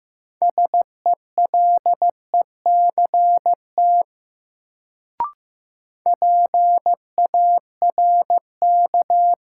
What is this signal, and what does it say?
Morse code, heard as a keyed tone, about 15 wpm, 700 Hz. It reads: SELECT PARK
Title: Sentences from Top 1300 Words - New Word Drill - 15wpm